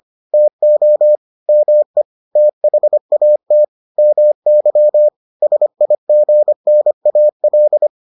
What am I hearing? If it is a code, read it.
Morse code, heard as a keyed tone, about 25 wpm, 600 Hz. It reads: TO ME THAT MY SIGNAL